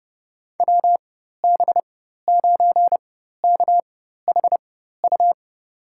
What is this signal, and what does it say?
Morse code, heard as a keyed tone, about 30 wpm, spaced as for 14 wpm, 700 Hz. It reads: WB9KHU
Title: USA Call Signs - 6 Characters - 4x Character Spacing - 30wpm